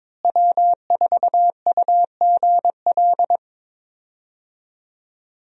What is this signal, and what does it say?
Morse code, heard as a keyed tone, about 22 wpm, 700 Hz. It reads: W4UGL
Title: USA Call Signs - 5 Characters - Repeated 3x - 22wpm